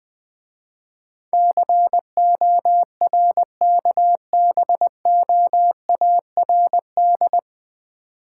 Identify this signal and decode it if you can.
Morse code, heard as a keyed tone, about 20 wpm, 700 Hz. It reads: CORKBOARD
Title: Difficult Words - Speed-Racing with Hints - 20wpm